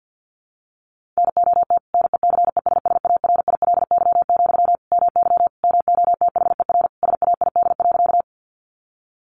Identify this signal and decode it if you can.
Morse code, heard as a keyed tone, about 50 wpm, 700 Hz. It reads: NOT NECESSARILY, MY MOTHER SAID.
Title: Sentences from Top 1400 Words 50wpm